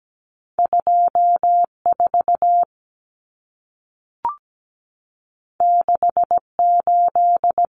Morse code, heard as a keyed tone, about 17 wpm, 700 Hz. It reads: 24 68
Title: Numbers - 2-Digit 17wpm